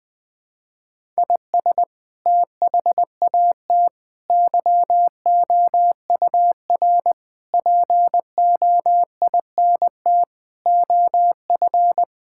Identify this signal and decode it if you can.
Morse code, heard as a keyed tone, about 20 wpm, 700 Hz. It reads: IS THAT YOUR POINT OF
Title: Sentences from Top 1000 Words - Review Part 6 - 20wpm